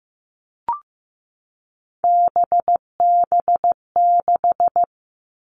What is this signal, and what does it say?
Morse code, heard as a keyed tone, about 15 wpm, 700 Hz. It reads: BB6
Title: All Cut Numbers - 3-Digit 15wpm